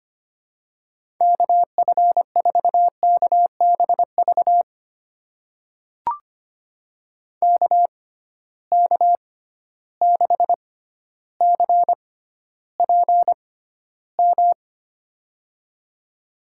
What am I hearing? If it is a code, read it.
Morse code, heard as a keyed tone, about 25 wpm, spaced as for 13 wpm, 700 Hz. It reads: KF4KBV KK6CPM
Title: USA Call Signs - 6 Characters - 6x Character Spacing - 25wpm